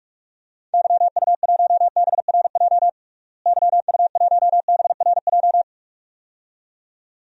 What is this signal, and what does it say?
Morse code, heard as a keyed tone, about 46 wpm, 700 Hz. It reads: YU1BRJ YU1BRJ
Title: DXCC-100 Call Signs - 6 Characters - Repeated 2x - 45wpm